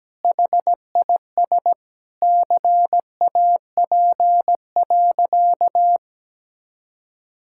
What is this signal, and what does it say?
Morse code, heard as a keyed tone, about 17 wpm, 700 Hz. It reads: HIS CAP.
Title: Sentences from Top 1600 Words 17wpm